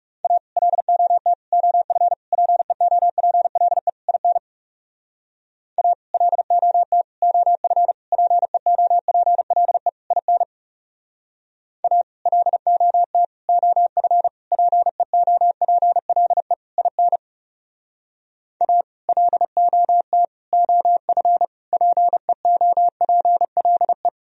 Morse code, heard as a keyed tone, about 36 wpm, 700 Hz. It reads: A LOT OF PEOPLE IN A LOT OF PEOPLE IN A LOT OF PEOPLE IN A LOT OF PEOPLE
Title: Speed-Racing - Sets of 5 Words 30wpm